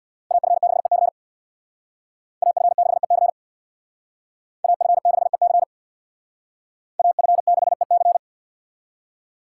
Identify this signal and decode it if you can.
Morse code, heard as a keyed tone, about 63 wpm, 700 Hz. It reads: AF6EC AF6EC AF6EC AF6EC